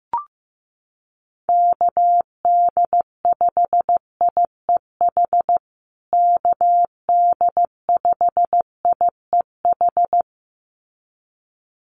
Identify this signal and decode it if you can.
Morse code, heard as a keyed tone, about 15 wpm, 700 Hz. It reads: KD5IEH KD5IEH
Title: USA Call Signs Repeated 2x 15wpm